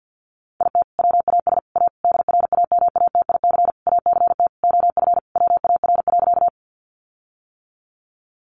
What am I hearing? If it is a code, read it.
Morse code, heard as a keyed tone, about 48 wpm, 700 Hz. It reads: IT WAS A DRAMATIC ACT OF WAR.
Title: Sentences from Top 1300 Words 50wpm